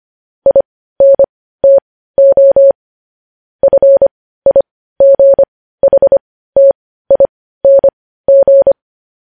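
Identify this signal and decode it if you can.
Morse code, heard as a keyed tone, about 25 wpm, spaced as for 15 wpm, 550 Hz. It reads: INTO FIGHTING
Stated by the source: Morse code news headlines at 15.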